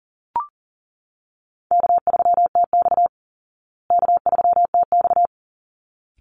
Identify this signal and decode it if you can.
Morse code, heard as a keyed tone, about 40 wpm, 700 Hz. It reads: K3TX K3TX
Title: Easy Call Signs Repeated 2x - 40wpm